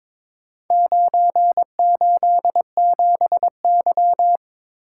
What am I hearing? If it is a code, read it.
Morse code, heard as a keyed tone, about 22 wpm, 690 Hz. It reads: 987Y
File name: practice / 22wpm / Four Letter-Number Combinations - 22wpm